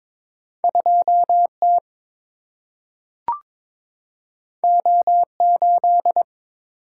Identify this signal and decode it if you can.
Morse code, heard as a keyed tone, about 22 wpm, 690 Hz. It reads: 2T O8
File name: practice / 22wpm / Two Letter-Number Combinations - 22wpm